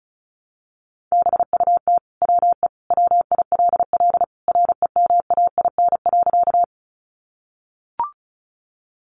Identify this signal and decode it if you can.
Morse code, heard as a keyed tone, about 34 wpm, 700 Hz. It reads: BUT WE WILL REMAIN.
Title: Sentences from Top 1200 Words 35wpm